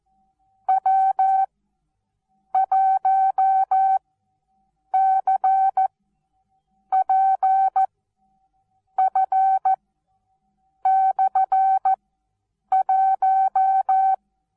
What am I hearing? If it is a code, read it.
Morse code, heard as a keyed tone, about 14 wpm, spaced as for 7 wpm, 770 Hz. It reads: W1CPF/1
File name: Onlinelehrgang / morsen / DJ4UF Morsen19